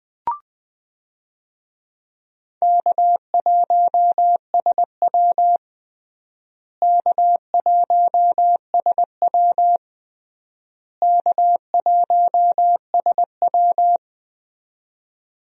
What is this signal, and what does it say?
Morse code, heard as a keyed tone, about 21 wpm, 700 Hz. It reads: K1SW K1SW K1SW
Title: USA Call Signs - 4 Characters - Repeated 3x - 20wpm